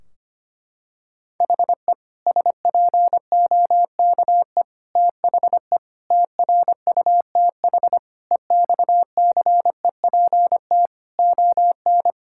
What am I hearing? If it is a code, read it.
Morse code, heard as a keyed tone, about 25 wpm, 700 Hz. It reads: HE SPOKE THE TRUTH EXCEPT ON
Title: Sentences from Top 1600 Words 25wpm